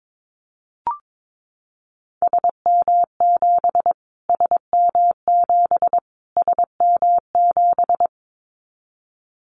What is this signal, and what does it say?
Morse code, heard as a keyed tone, about 22 wpm, 700 Hz. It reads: SM7 SM7 SM7
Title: Callsign Prefixes for Top 25 DXCC - Repeated 3x - 22wpm